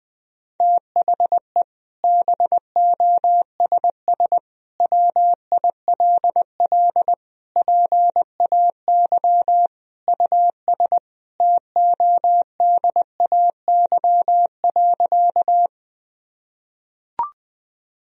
Morse code, heard as a keyed tone, about 21 wpm, 700 Hz. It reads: THE BOSS WILL PAY US TODAY.
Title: Sentences from Top 1000 Words - Review Part 4 - 20wpm